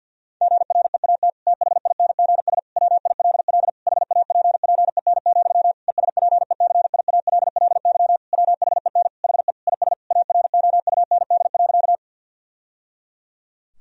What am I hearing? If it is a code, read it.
Morse code, heard as a keyed tone, about 50 wpm, 700 Hz. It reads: GREAT THINGS WILL HAPPEN, ESPECIALLY WHEN HE IS AROUND.
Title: Sentences from Top 1000 Words - Review Part 4 - 50wpm